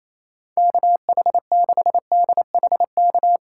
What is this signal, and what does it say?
Morse code, heard as a keyed tone, about 27 wpm, 700 Hz. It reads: KH6DHK
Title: K1USN SST QSO Practice - 28wpm